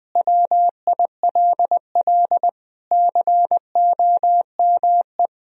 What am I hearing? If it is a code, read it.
Morse code, heard as a keyed tone, about 20 wpm, 700 Hz. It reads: WILL COME
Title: Sentences from Top 1000 Words - Review Part 2 - 20wpm